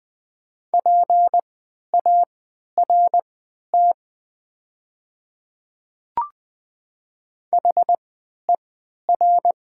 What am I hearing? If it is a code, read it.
Morse code, heard as a keyed tone, about 20 wpm, spaced as for 11 wpm, 700 Hz. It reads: PART HER
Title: 3x Character Spacing - Top 200 Words - 20wpm